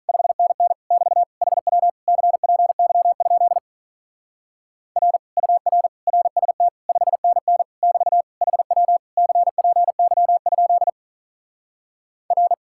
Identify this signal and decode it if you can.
Morse code, heard as a keyed tone, about 42 wpm, 700 Hz. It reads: 5NN <BT> HW CPY? R UR RST 5NN <BT> HW CPY? R
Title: Speed-Racing Common QSO Phrases - 35wpm